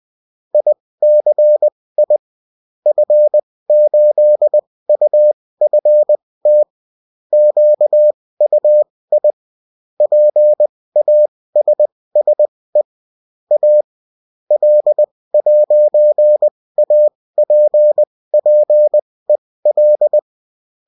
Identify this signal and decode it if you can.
Morse code, heard as a keyed tone, about 20 wpm, spaced as for 16 wpm, 600 Hz. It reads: ICI F8UFT QUI PASSE A L'APPEL
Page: bulletin-uft-sous-forme-texte-et-en-cw